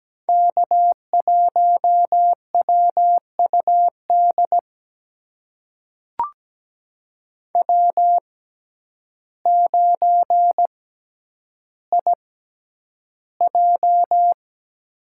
Morse code, heard as a keyed tone, about 17 wpm, spaced as for 9 wpm, 700 Hz. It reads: K1WUD W9IJ
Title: USA Call Signs - 6x Character Spacing - 17wpm